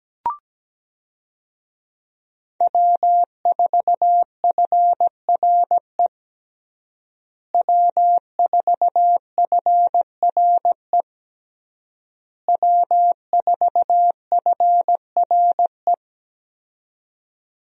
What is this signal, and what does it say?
Morse code, heard as a keyed tone, about 17 wpm, 700 Hz. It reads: W4FRE W4FRE W4FRE
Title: USA Call Signs - 5 Characters - Repeated 3x - 17wpm